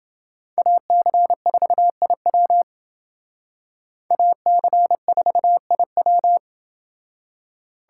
Morse code, heard as a keyed tone, about 29 wpm, 700 Hz. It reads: AC4IW AC4IW